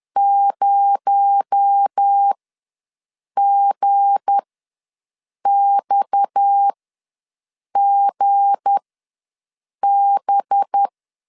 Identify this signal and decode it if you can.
Morse code, heard as a keyed tone, about 10 wpm, spaced as for 6 wpm, 800 Hz. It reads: 0GXGB
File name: LEZIONI / 7 GF9Z / 7 G ws 10wpm 3843